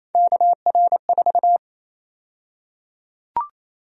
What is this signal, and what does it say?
Morse code, heard as a keyed tone, about 28 wpm, 700 Hz. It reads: KR4
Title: USA Call Sign Prefixes - 28wpm